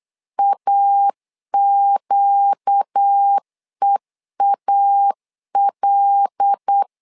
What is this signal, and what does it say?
Morse code, heard as a keyed tone, about 8 wpm, 800 Hz. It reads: AQEAL
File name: LEZIONI / 5 LBVC / 5 L 8wpm 2831